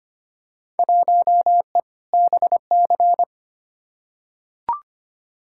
Morse code, heard as a keyed tone, about 25 wpm, 700 Hz. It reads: 1E BC
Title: Field Day Exchanges - 25wpm